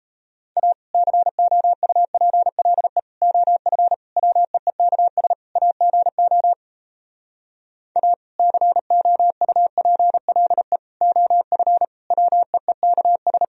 Morse code, heard as a keyed tone, about 37 wpm, 700 Hz. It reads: A COUPLE OF WEEKS AGO A COUPLE OF WEEKS